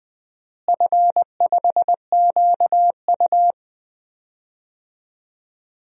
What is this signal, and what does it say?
Morse code, heard as a keyed tone, about 19 wpm, 700 Hz. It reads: F5QU
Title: DXCC-100 Call Signs - 4 Characters - Repeated 3x - 20wpm